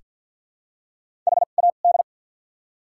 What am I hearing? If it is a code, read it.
Morse code, heard as a keyed tone, about 48 wpm, spaced as for 35 wpm, 700 Hz. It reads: HAD
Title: 2x Character Spacing - Top 300 Words - 50wpm